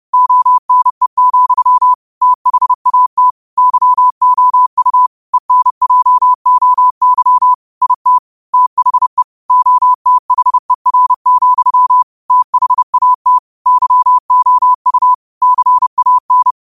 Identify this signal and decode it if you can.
Morse code, heard as a keyed tone, about 30 wpm, 1000 Hz. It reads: ONE, THAT YOU ENJOY IT THE OTHER, THAT YOU CAN
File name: qotdfiles / 30wpm